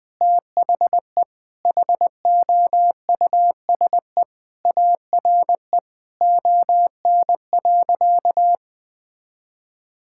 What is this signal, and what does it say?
Morse code, heard as a keyed tone, about 21 wpm, 700 Hz. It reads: THE HOUSE ARE ON.
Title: Sentences with 2-Word Phrases - Spoken Beginning - 20wpm